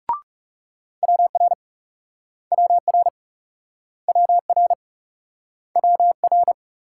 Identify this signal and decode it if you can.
Morse code, heard as a keyed tone, about 40 wpm, 700 Hz. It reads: WR WR WR WR